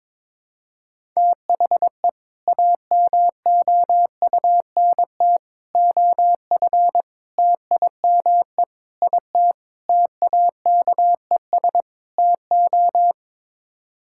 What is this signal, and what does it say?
Morse code, heard as a keyed tone, about 22 wpm, 700 Hz. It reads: THE AMOUNT OF TIME IT TAKES TO
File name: practice / 22wpm / Sets of 7 Words - 22wpm